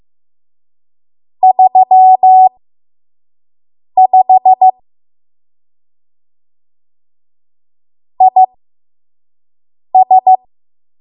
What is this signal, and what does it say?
Morse code, heard as a keyed tone, about 15 wpm, spaced as for 5 wpm, 750 Hz. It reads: 35 IS
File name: Morse / 05WPM